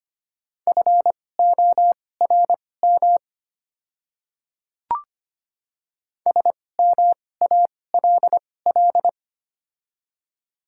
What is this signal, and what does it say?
Morse code, heard as a keyed tone, about 25 wpm, spaced as for 18 wpm, 700 Hz. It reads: FORM SMALL
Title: 2x Character Spacing - Top 300 Words - 25wpm